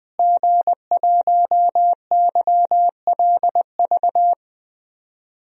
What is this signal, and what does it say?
Morse code, heard as a keyed tone, about 21 wpm, 700 Hz. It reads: G1YLV
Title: DXCC-100 Call Signs - 5 Characters - Repeated 3x - 20wpm